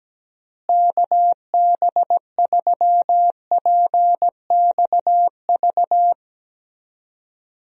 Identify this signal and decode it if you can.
Morse code, heard as a keyed tone, about 17 wpm, 700 Hz. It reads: KB3PXV
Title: USA Call Signs - 6 Characters - Repeated 3x - 17wpm